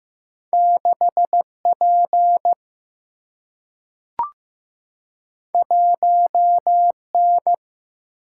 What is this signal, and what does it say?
Morse code, heard as a keyed tone, about 15 wpm, 700 Hz. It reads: 6 P 1 N
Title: Single-Letter Single-Number 15wpm